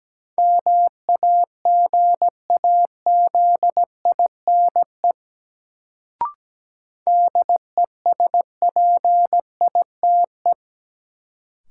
Morse code, heard as a keyed tone, about 17 wpm, 700 Hz. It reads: MAGAZINE DESPITE
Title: Sentences from Top 1000 Words - Review Part 7 - New Word Drill - 17wpm